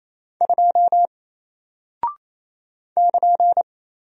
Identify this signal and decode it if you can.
Morse code, heard as a keyed tone, about 27 wpm, 700 Hz. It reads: 2 <KN>
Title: Single Character (Letter:Number:Punctuation:Prosign) Alt 28wpm